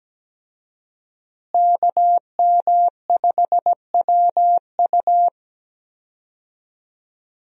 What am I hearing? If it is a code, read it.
Morse code, heard as a keyed tone, about 17 wpm, 700 Hz. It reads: KM5WU
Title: USA Call Signs - 5 Characters - Repeated 2x - 17wpm